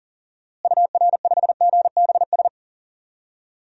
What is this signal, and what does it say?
Morse code, heard as a keyed tone, about 40 wpm, 700 Hz. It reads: UR5GBS
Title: DXCC-100 Call Signs - 6 Characters - Repeated 2x - 40wpm